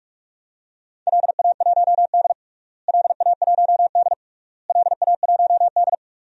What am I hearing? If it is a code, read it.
Morse code, heard as a keyed tone, about 46 wpm, 700 Hz. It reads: LA1D LA1D LA1D
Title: DXCC-100 Call Signs - 4 Characters - Repeated 3x - 45wpm